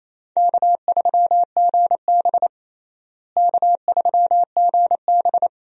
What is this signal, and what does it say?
Morse code, heard as a keyed tone, about 28 wpm, 700 Hz. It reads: K3GB K3GB
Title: USA Call Signs - 4 Characters - Repeated 3x - 28wpm